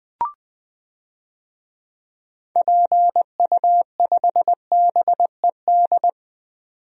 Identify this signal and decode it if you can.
Morse code, heard as a keyed tone, about 20 wpm, 700 Hz. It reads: PU5BED
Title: DXCC-100 Call Signs - 6 Characters - 20wpm